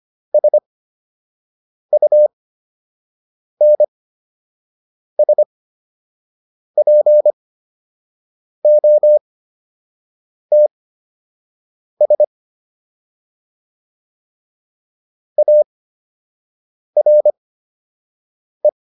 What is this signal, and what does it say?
Morse code, heard as a keyed tone, about 25 wpm, spaced as for 6 wpm, 600 Hz. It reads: SUNSPOTS ARE